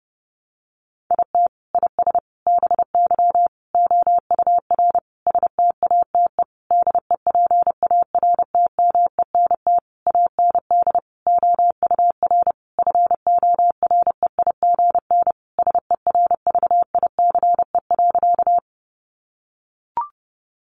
Morse code, heard as a keyed tone, about 30 wpm, 700 Hz. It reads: IT IS BY OUR STATE DEPARTMENT AND OUR FOREIGN SERVICE.
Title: Sentences from Top 1400 Words 30wpm